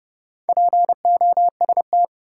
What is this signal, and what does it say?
Morse code, heard as a keyed tone, about 32 wpm, 700 Hz. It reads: POST
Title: Four Letter Words - 30wpm